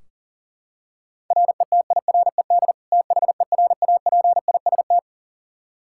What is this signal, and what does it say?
Morse code, heard as a keyed tone, about 37 wpm, 700 Hz. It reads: RETIRED THERAPIST
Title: Practice QSO - Exchange 3 - More Info - 40wpm